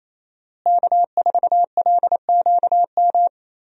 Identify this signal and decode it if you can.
Morse code, heard as a keyed tone, about 27 wpm, 700 Hz. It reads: K4LQM
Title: USA Call Signs - 5 Characters - Repeated 3x - 28wpm